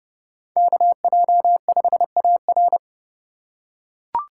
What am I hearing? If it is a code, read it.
Morse code, heard as a keyed tone, about 30 wpm, 700 Hz. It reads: KJ5AR
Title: USA Call Signs - 5 Characters - 2x Character Spacing - 30wpm